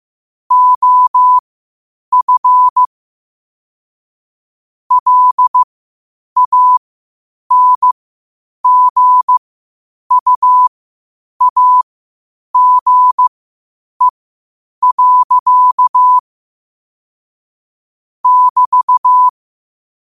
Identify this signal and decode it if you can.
Morse code, heard as a keyed tone, about 15 wpm, spaced as for 8 wpm, 1000 Hz. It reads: OF LANGUAGE. <BT>
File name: qotdfiles / 8wpm